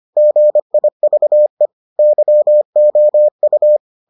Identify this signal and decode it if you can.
Morse code, heard as a keyed tone, about 25 wpm, 600 Hz. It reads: GIVE YOU